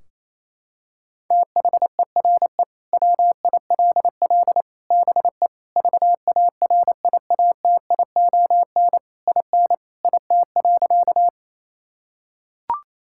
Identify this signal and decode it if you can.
Morse code, heard as a keyed tone, about 28 wpm, 700 Hz. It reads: THERE WILL BE VARIATION IN IT.
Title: Sentences from Top 1600 Words 28wpm